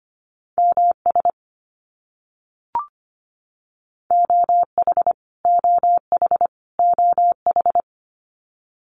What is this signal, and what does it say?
Morse code, heard as a keyed tone, about 25 wpm, 700 Hz. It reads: MS OH OH OH
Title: US States Repeated 3x 25wpm